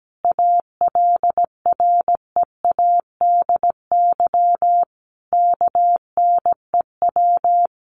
Morse code, heard as a keyed tone, about 17 wpm, 700 Hz. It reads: ALREADY KNEW
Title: Sentences from Top 700 Words 17wpm